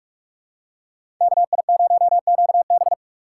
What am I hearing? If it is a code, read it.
Morse code, heard as a keyed tone, about 46 wpm, 700 Hz. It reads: KI0QB
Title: USA Call Signs - 3x Character Spacing - 45wpm